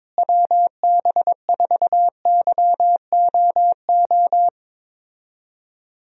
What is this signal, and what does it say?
Morse code, heard as a keyed tone, about 22 wpm, 700 Hz. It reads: WB4YOO
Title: USA Call Signs - 6 Characters - Repeated 3x - 22wpm